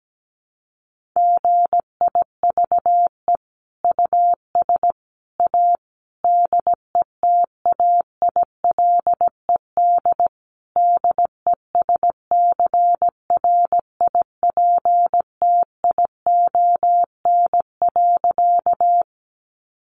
Morse code, heard as a keyed tone, about 17 wpm, 700 Hz. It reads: GIVE US A DETAILED DESCRIPTION.
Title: Sentences from Top 1500 Words 17wpm